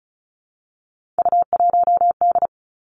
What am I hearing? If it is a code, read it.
Morse code, heard as a keyed tone, about 36 wpm, 700 Hz. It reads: U1D
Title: Three Challenging Letter-Number Combinations 35wpm